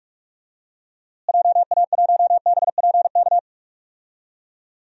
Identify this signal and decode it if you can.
Morse code, heard as a keyed tone, about 47 wpm, 700 Hz. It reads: JA1BPK
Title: DXCC-100 Call Signs - 6 Characters - Repeated 3x - 45wpm